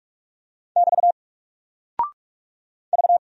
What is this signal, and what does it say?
Morse code, heard as a keyed tone, about 44 wpm, 700 Hz. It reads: <BT> V
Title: Single Character (Letter:Number:Punctuation:Prosign) 45wpm